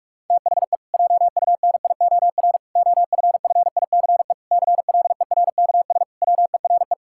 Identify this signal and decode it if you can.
Morse code, heard as a keyed tone, about 41 wpm, 700 Hz. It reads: THE JUNIOR OFFICE CLERKS WERE
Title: Pangram Challenge v2 - 45wpm